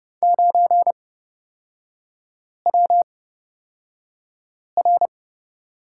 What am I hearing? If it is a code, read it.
Morse code, heard as a keyed tone, about 30 wpm, spaced as for 5 wpm, 700 Hz. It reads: 9WR
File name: Lesson 36e - BT - Three Characters 30wpm